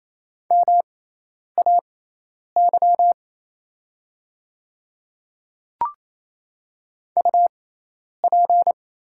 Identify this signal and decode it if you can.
Morse code, heard as a keyed tone, about 28 wpm, spaced as for 10 wpm, 690 Hz. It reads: MAY UP
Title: 6x Character Spacing - Top 100 Words - 28wpm